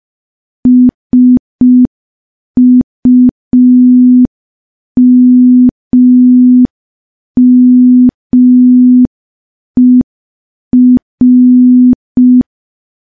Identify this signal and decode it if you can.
Morse code, heard as a keyed tone, about 5 wpm, 250 Hz. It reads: SUMMER
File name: morse / morse-training-05WPM